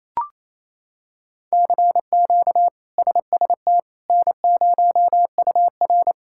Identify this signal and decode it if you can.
Morse code, heard as a keyed tone, about 28 wpm, 700 Hz. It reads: CQ SST N0UR
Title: K1USN SST QSO Practice - 28wpm